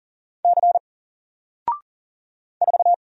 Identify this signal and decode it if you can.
Morse code, heard as a keyed tone, about 39 wpm, 700 Hz. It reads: C 4
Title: Single Letter-Number 40wpm